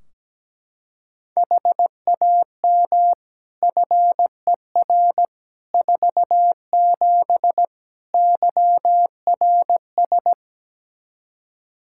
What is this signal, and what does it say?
Morse code, heard as a keyed tone, about 17 wpm, 700 Hz. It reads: HAM FER 47 YRS
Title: K1USN SST QSO Practice 17wpm